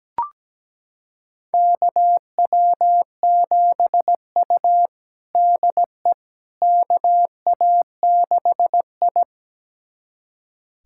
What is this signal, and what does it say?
Morse code, heard as a keyed tone, about 17 wpm, 700 Hz. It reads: KW7U DE KA6I
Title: USA Callsign DE Callsign - 4 Characters - 17wpm